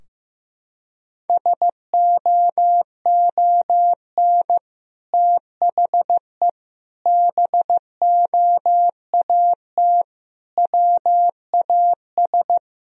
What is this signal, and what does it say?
Morse code, heard as a keyed tone, about 15 wpm, 700 Hz. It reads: SOON THE BOAT WAS
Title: Sentences from Top 1000 Words - Review Part 3 - 15wpm